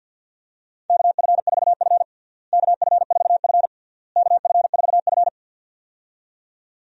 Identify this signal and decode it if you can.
Morse code, heard as a keyed tone, about 52 wpm, 700 Hz. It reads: KF4F KF4F KF4F
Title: USA Call Signs - 4 Characters - Repeated 3x - 50wpm